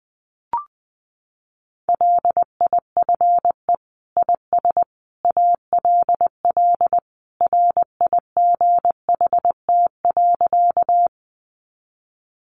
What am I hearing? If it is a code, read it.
Morse code, heard as a keyed tone, about 20 wpm, 700 Hz. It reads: LIFE IS ALL RIGHT.
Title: Sentences from Top 300 Words 20wpm